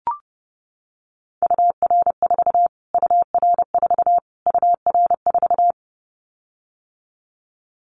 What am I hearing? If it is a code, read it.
Morse code, heard as a keyed tone, about 30 wpm, 700 Hz. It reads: UR4 UR4 UR4
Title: Callsign Prefixes for Top 25 DXCC - Repeated 3x - 30wpm